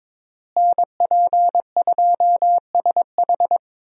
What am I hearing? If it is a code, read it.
Morse code, heard as a keyed tone, about 22 wpm, 700 Hz. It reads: NP2SH
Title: USA Call Signs - 5 Characters - Repeated 2x - 22wpm